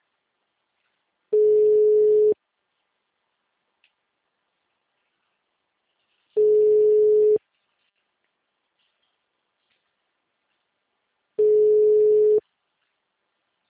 Ringback tone (Europe, 425 Hz).
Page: Itu - SP